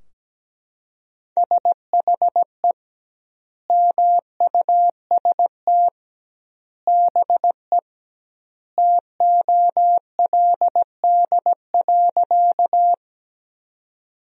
Morse code, heard as a keyed tone, about 17 wpm, 700 Hz. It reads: SHE MUST BE TOLD.
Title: 2x Word-Spacing - Sentences from Top 400 Words - 17wpm